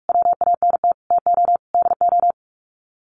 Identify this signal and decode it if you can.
Morse code, heard as a keyed tone, about 46 wpm, 700 Hz. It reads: WANT TO DO
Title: Sets of 5 Words Encore 45wpm